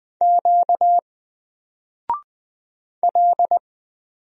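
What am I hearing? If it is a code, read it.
Morse code, heard as a keyed tone, about 20 wpm, 700 Hz. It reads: Q L
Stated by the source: Single letter and number Morse drill at 20 wpm, from N4LCD.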